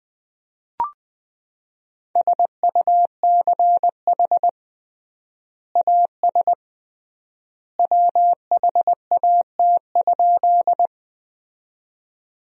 Morse code, heard as a keyed tone, about 20 wpm, 700 Hz. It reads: SUCH AS WHAT?
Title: 3x Word-Spacing - Sentences from Top 200 Words - 20wpm